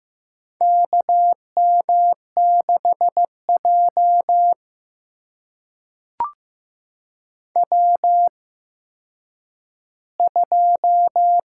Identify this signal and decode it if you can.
Morse code, heard as a keyed tone, about 15 wpm, spaced as for 9 wpm, 700 Hz. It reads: KM6J W2
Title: USA Call Signs - 8x Character Spacing - 15wpm